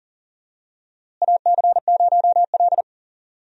Morse code, heard as a keyed tone, about 40 wpm, 700 Hz. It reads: AC0L
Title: USA Call Signs - 4 Characters - Repeated 3x - 40wpm